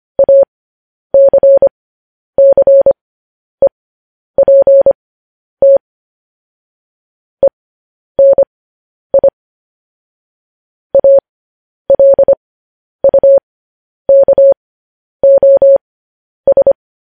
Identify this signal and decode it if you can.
Morse code, heard as a keyed tone, about 25 wpm, spaced as for 10 wpm, 550 Hz. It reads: ACCEPT ENI ALUKOS
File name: podcast / News headlines-1x-25x10